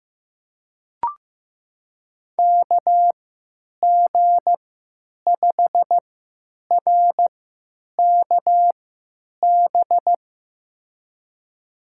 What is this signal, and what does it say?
Morse code, heard as a keyed tone, about 15 wpm, spaced as for 9 wpm, 700 Hz. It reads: KG5RKB
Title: USA Call Signs - 6 Characters - 3x Character Spacing - 15wpm